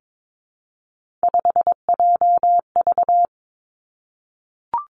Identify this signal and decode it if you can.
Morse code, heard as a keyed tone, about 22 wpm, 700 Hz. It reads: 5JV
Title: Three Challenging Letter-Number Combinations 22wpm